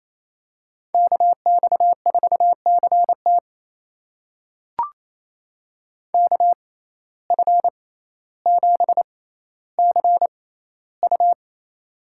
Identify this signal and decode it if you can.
Morse code, heard as a keyed tone, about 28 wpm, spaced as for 14 wpm, 700 Hz. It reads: KX4CT KF7CU
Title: USA Call Signs - 5 Characters - 6x Character Spacing - 28wpm